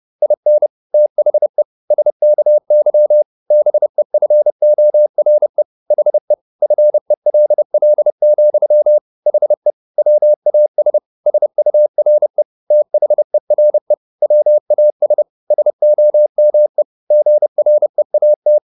Morse code, heard as a keyed tone, about 30 wpm, 600 Hz. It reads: IN THE SKY BEFORE HE FELL, HE WAS SURE THERE WAS SOME GREAT